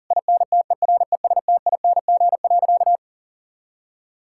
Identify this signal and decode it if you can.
Morse code, heard as a keyed tone, about 40 wpm, 700 Hz. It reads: INTERESTING.
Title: Sentences from Top 1000 Words - Review Part 5 - 40wpm